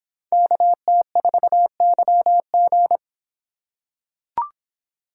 Morse code, heard as a keyed tone, about 24 wpm, 700 Hz. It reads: KT4YG